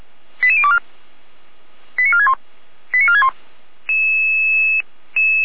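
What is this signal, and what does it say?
Five-tone selective call (ZVEI), heard as quick runs of five tones, tones of about 70 ms. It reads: 88024 89461 89461
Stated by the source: ZVEI five-tone call received on SDR, from the feuerwehr-orlamuende.org fire brigade audio archive.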